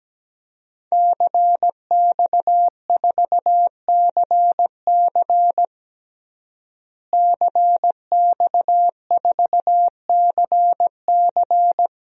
Morse code, heard as a keyed tone, about 17 wpm, 700 Hz. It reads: CX4CC CX4CC
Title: DXCC-100 Call Signs - 5 Characters - Repeated 2x - 17wpm